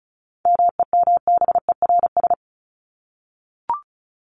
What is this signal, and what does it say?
Morse code, heard as a keyed tone, about 34 wpm, 700 Hz. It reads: MEMBERS
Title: ES:S Suffix - Top 100 Words with Base Words 35wpm